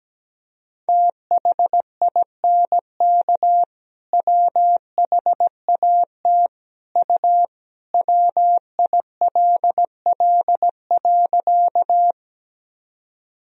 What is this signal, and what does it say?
Morse code, heard as a keyed tone, about 17 wpm, 700 Hz. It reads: THINK WHAT U WILL.
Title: Sentences from Top 200 Words and Abbreviations 17wpm